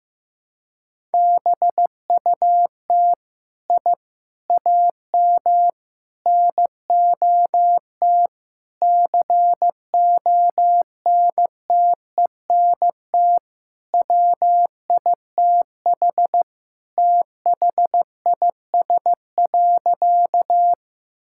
Morse code, heard as a keyed tone, about 15 wpm, 700 Hz. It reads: BUT I AM NOT CONTENT WITH THIS.
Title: Sentences from Top 1000 Words - Review Part 6 - 15wpm